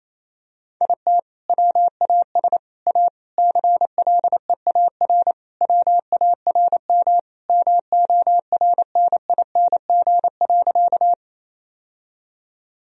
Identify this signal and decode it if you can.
Morse code, heard as a keyed tone, about 28 wpm, 700 Hz. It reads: IT WAS A CLEAR WARM MORNING.
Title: Sentences from Top 1000 Words - Review Part 2 - 28wpm